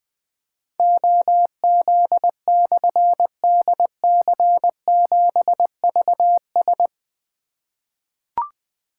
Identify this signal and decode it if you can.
Morse code, heard as a keyed tone, about 20 wpm, 700 Hz. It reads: OZ/DC7VS
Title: Difficult Call Signs 20wpm